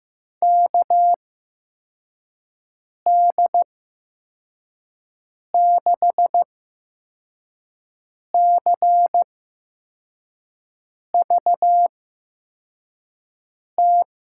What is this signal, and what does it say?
Morse code, heard as a keyed tone, about 15 wpm, spaced as for 4 wpm, 700 Hz. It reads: KD6CVT
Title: USA Call Signs - 8x Character Spacing - 15wpm